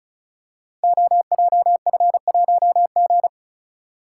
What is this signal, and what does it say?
Morse code, heard as a keyed tone, about 36 wpm, 700 Hz. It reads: OJF1G
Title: Code Groups - 5 Letter:Numbers - 35wpm